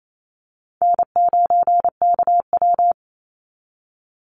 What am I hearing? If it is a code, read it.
Morse code, heard as a keyed tone, about 28 wpm, 700 Hz. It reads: N9KW
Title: NA SOTA Chaser Callsigns 28wpm